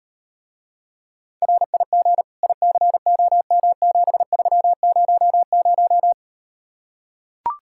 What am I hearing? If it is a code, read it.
Morse code, heard as a keyed tone, about 38 wpm, 700 Hz. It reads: RIG ICOM7300